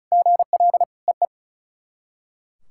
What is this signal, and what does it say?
Morse code, heard as a keyed tone, about 34 wpm, 690 Hz. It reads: GL EE
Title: SOTA QSO Practice 35wpm